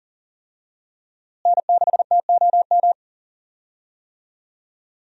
Morse code, heard as a keyed tone, about 40 wpm, 700 Hz. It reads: N6TOM
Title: USA Call Signs - 5 Characters - Repeated 2x - 40wpm